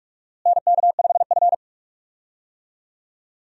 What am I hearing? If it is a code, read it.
Morse code, heard as a keyed tone, about 46 wpm, 700 Hz. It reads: NK5F
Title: USA Call Signs - 4 Characters - Repeated 3x - 45wpm